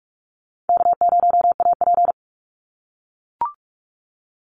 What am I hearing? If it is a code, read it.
Morse code, heard as a keyed tone, about 44 wpm, 700 Hz. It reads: K0AP
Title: Call Signs - 45wpm